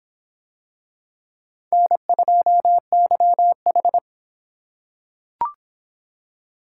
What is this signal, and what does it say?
Morse code, heard as a keyed tone, about 26 wpm, 700 Hz. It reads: N2YH